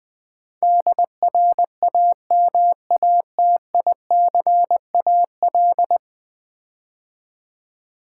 Morse code, heard as a keyed tone, about 20 wpm, 700 Hz. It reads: DRAMATICAL
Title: Build A Word - 5 Letters At A Time - 12-Letter Words - 20wpm